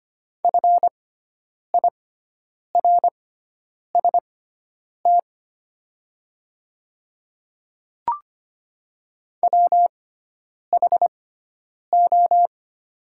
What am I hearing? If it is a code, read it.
Morse code, heard as a keyed tone, about 25 wpm, spaced as for 9 wpm, 700 Hz. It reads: FIRST WHO
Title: 6x Character Spacing - Top 100 Words - 25wpm